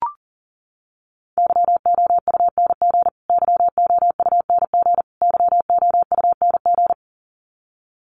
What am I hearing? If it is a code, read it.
Morse code, heard as a keyed tone, about 40 wpm, 700 Hz. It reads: YOUNG YOUNG YOUNG
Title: Top 500 Words Repeated 40wpm